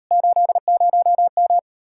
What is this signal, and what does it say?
Morse code, heard as a keyed tone, about 39 wpm, 700 Hz. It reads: 80M